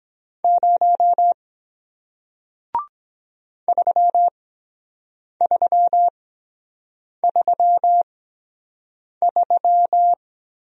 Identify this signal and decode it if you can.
Morse code, heard as a keyed tone, about 22 wpm, spaced as for 7 wpm, 700 Hz. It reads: 0 3333